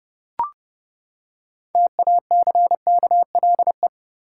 Morse code, heard as a keyed tone, about 32 wpm, 700 Hz. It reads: TACKLE
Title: Six Letter Words - 30wpm